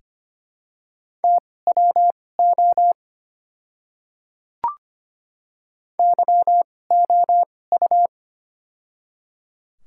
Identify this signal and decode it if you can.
Morse code, heard as a keyed tone, about 25 wpm, spaced as for 18 wpm, 700 Hz. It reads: TWO YOU
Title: 2x Character Spacing - Top 300 Words - 25wpm